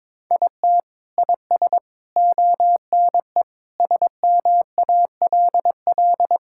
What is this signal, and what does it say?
Morse code, heard as a keyed tone, about 21 wpm, 700 Hz. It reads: IT IS ONE SMALL
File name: Sentences with 9-Word Phrases - 22wpm